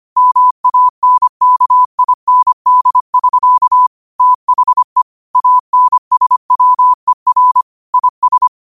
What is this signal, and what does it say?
Morse code, heard as a keyed tone, about 25 wpm, 1000 Hz. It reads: MANKIND<SK> THE ANSWER IS
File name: qotdfiles / 25wpm